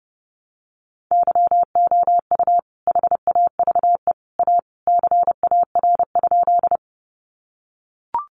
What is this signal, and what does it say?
Morse code, heard as a keyed tone, about 30 wpm, 700 Hz. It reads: YOU HAVE A CAR?
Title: Sentences from Top 300 Words 30wpm